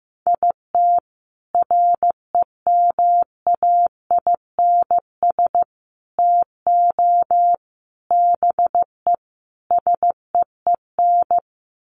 Morse code, heard as a keyed tone, about 15 wpm, 700 Hz. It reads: IT REMAINS TO BE SEEN
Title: Sets of 5 Words Encore 15wpm